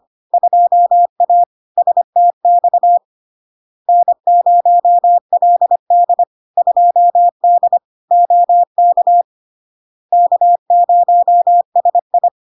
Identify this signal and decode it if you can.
Morse code, heard as a keyed tone, about 25 wpm, 700 Hz. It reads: 2A STX N0LD 2D OK K0SI
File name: LARK-FD-2022-Log-in-CW-25wpm-700-Hz